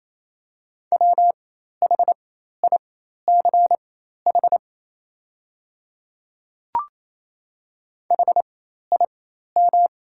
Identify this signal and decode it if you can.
Morse code, heard as a keyed tone, about 28 wpm, spaced as for 13 wpm, 700 Hz. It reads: WHICH HIM
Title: 4x Character Spacing - Top 100 Words - 28wpm